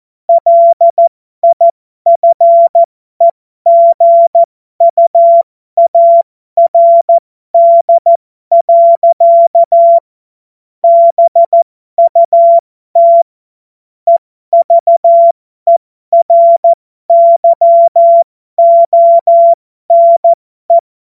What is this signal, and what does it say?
Morse code, heard as a keyed tone, about 14 wpm, spaced as for 12 wpm, 680 Hz. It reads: LIFEGUARD. BUT EVERYONE
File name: Morsekurs / Witze